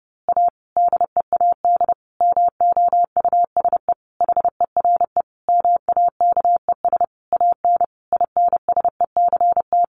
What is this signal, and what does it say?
Morse code, heard as a keyed tone, about 30 wpm, 700 Hz. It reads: A DEAD MOUSE HERE MAKES AN INSECT
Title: A Sound Of Thunder 30wpm